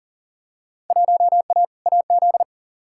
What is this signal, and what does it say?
Morse code, heard as a keyed tone, about 39 wpm, 700 Hz. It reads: 1A AZ
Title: Field Day Exchanges - 40wpm